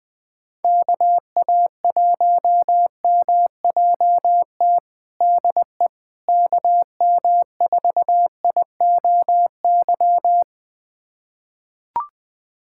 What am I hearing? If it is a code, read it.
Morse code, heard as a keyed tone, about 21 wpm, 700 Hz. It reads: KA1MJT DE KM4IOY
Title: USA Callsign DE Callsign - 6 Characters - 20wpm